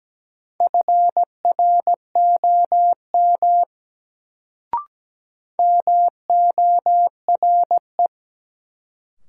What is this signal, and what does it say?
Morse code, heard as a keyed tone, about 17 wpm, 700 Hz. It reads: FROM MORE
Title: Sentences from Top 100 Words - New Word Drill - 17wpm